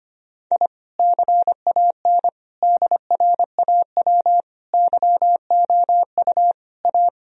Morse code, heard as a keyed tone, about 25 wpm, 700 Hz. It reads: I CAN DRAW YOU A
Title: Sentences from Top 1000 Words - Review Part 1 - 25wpm